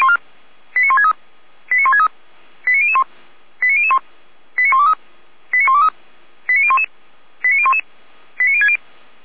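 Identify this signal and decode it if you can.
Five-tone selective call (ZVEI), heard as quick runs of five tones, tones of about 80 ms. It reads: 89163 89163 89001 89001 89123 89123 89011 89011 89060
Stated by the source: ZVEI five-tone call received by SDR.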